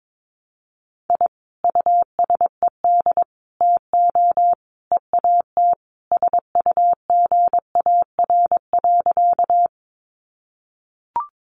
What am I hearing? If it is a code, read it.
Morse code, heard as a keyed tone, about 22 wpm, 700 Hz. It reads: I USED TO EAT SUGAR.
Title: Sentences from Top 1200 Words 22wpm